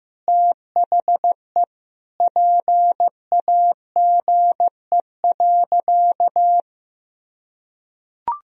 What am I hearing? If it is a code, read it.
Morse code, heard as a keyed tone, about 15 wpm, 700 Hz. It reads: THE PAGE.
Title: Sentences from Top 1000 Words - Review Part 2 - 15wpm